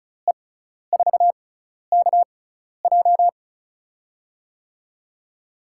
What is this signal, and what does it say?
Morse code, heard as a keyed tone, about 36 wpm, spaced as for 12 wpm, 700 Hz. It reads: E4KJ
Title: USA Call Signs - 5 Characters - 6x Character Spacing - 35wpm